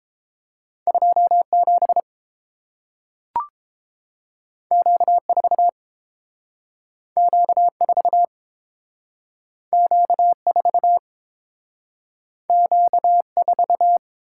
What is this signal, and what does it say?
Morse code, heard as a keyed tone, about 30 wpm, 700 Hz. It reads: 2 7 Q 4 Q 4 Q 4 Q 4